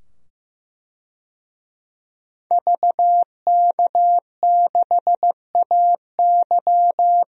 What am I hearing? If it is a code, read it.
Morse code, heard as a keyed tone, about 15 wpm, 700 Hz. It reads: VK6AY
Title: DXCC-100 Call Signs - 5 Characters - 15wpm